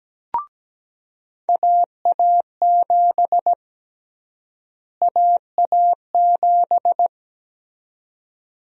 Morse code, heard as a keyed tone, about 17 wpm, 700 Hz. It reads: AA7 AA7
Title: USA Call Sign Prefixes - Repeated 2x - 17wpm